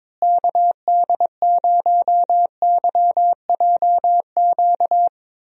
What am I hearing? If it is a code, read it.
Morse code, heard as a keyed tone, about 22 wpm, 700 Hz. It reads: KD0YJQ
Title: USA Call Signs - 6 Characters - Repeated 3x - 22wpm